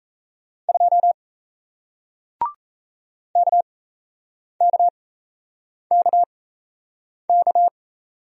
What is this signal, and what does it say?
Morse code, heard as a keyed tone, about 38 wpm, spaced as for 8 wpm, 700 Hz. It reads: 2 KKKK